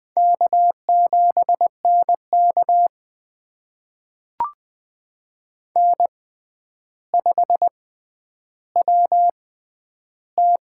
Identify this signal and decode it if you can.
Morse code, heard as a keyed tone, about 20 wpm, spaced as for 10 wpm, 700 Hz. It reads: K7NK N5WT
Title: USA Call Signs - 4 Characters - 6x Character Spacing - 20wpm